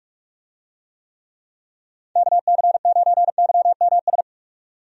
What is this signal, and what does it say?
Morse code, heard as a keyed tone, about 42 wpm, 700 Hz. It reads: KC9YMS
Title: USA Call Signs - 6 Characters - 45wpm